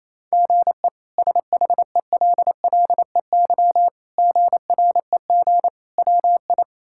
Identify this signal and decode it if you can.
Morse code, heard as a keyed tone, about 28 wpm, 700 Hz. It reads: GE SHELLEY GREG WI
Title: K1USN SST QSO Practice - 28wpm